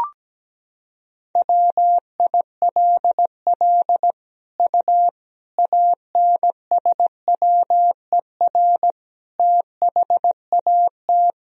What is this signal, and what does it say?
Morse code, heard as a keyed tone, about 17 wpm, 700 Hz. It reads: WILL U ANSWER THAT
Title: Sentences from Top 400 Words and Abbreviations 17wpm